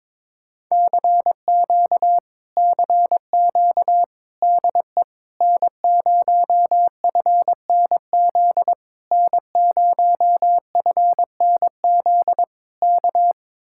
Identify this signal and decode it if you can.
Morse code, heard as a keyed tone, about 22 wpm, 700 Hz. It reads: CQ CQ DE N0FNZ N0FNZ K
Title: Speed-Racing Common QSO Phrases - 22wpm